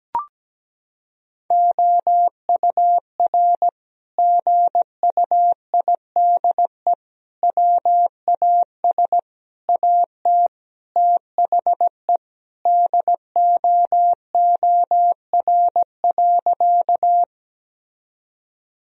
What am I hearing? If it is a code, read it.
Morse code, heard as a keyed tone, about 17 wpm, 700 Hz. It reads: OUR GUIDE WAS AT THE DOOR.
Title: Sentences from Top 1000 Words - Review Part 3 - 17wpm